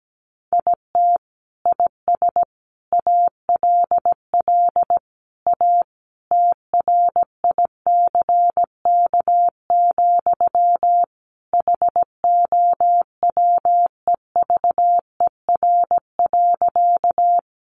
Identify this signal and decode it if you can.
Morse code, heard as a keyed tone, about 17 wpm, 700 Hz. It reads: IT IS ALL A TRICK, HOWEVER.
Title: Sentences from Top 1500 Words 17wpm